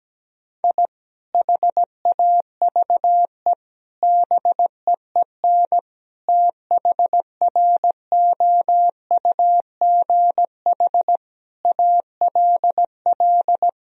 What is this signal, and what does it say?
Morse code, heard as a keyed tone, about 17 wpm, 700 Hz. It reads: I HAVE BEEN THROUGH ALL
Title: Sentences from Top 1000 Words - Review Part 1 - 17wpm